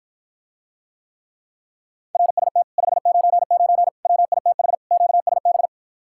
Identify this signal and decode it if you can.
Morse code, heard as a keyed tone, about 53 wpm, 700 Hz. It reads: RST 599 WITH QSB